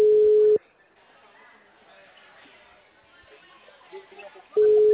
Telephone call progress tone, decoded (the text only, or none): ringback tone